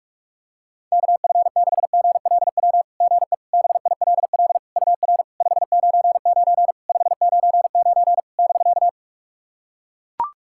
Morse code, heard as a keyed tone, about 44 wpm, 700 Hz. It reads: KF6GLW GE BILL UR 599 599 <BK>
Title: SOTA QSO Practice 45wpm